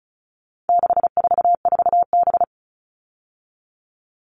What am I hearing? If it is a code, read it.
Morse code, heard as a keyed tone, about 34 wpm, 700 Hz. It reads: 644B
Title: All Cut Numbers - 4-Digit 35wpm